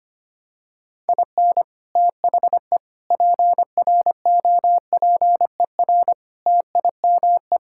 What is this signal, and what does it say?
Morse code, heard as a keyed tone, about 25 wpm, 700 Hz. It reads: IN THE PROPER TIME
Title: Sentences from Top 1000 Words - Review Part 5 - 25wpm